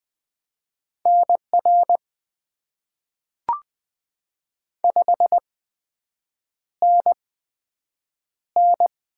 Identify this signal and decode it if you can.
Morse code, heard as a keyed tone, about 20 wpm, spaced as for 7 wpm, 700 Hz. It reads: NR 5NN
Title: Top 90 QSO Elements - 8x Character Spacing - 20wpm